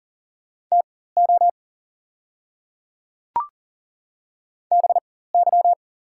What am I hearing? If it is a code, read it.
Morse code, heard as a keyed tone, about 38 wpm, spaced as for 18 wpm, 700 Hz. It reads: TO BY
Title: 4x Character Spacing - Top 100 Words - 40wpm